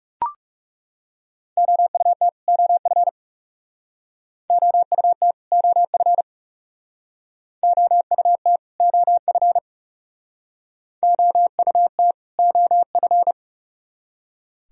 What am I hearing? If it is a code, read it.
Morse code, heard as a keyed tone, about 39 wpm, 700 Hz. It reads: OUT OF OUT OF OUT OF OUT OF